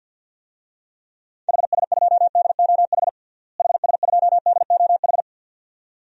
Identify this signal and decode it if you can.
Morse code, heard as a keyed tone, about 46 wpm, 700 Hz. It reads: HS2DOH HS2DOH
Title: DXCC-100 Call Signs - 6 Characters - Repeated 2x - 50wpm